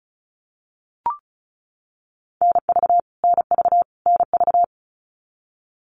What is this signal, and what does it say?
Morse code, heard as a keyed tone, about 34 wpm, 700 Hz. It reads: NV NV NV
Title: US States Repeated 3x 35wpm